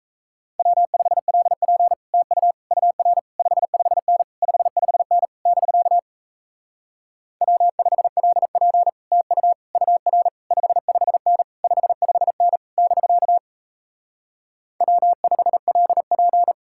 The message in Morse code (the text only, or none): W5LP TU UR 55N 55N <BK> W5LP TU UR 55N 55N <BK> W5LP